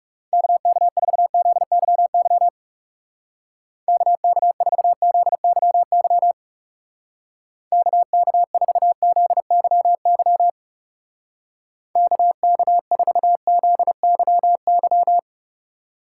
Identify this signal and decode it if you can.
Morse code, heard as a keyed tone, about 37 wpm, 700 Hz. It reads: KK4ZYY KK4ZYY KK4ZYY KK4ZYY
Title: Speed Racing - USA 2x3 Call Signs 30wpm